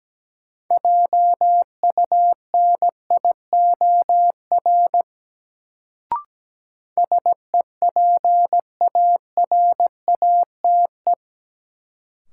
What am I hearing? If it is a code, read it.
Morse code, heard as a keyed tone, about 17 wpm, 700 Hz. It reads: JUNIOR SEPARATE
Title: Sentences from Top 1000 Words - Review Part 7 - New Word Drill - 17wpm